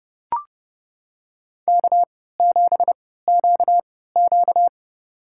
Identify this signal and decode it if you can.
Morse code, heard as a keyed tone, about 30 wpm, spaced as for 17 wpm, 700 Hz. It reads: K7QQ
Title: USA Call Signs - 4 Characters - 3x Character Spacing - 30wpm